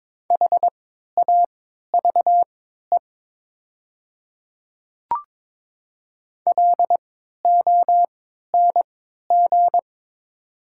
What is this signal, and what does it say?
Morse code, heard as a keyed tone, about 22 wpm, spaced as for 12 wpm, 690 Hz. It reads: HAVE LONG
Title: 3x Character Spacing - Top 100 Words - 22wpm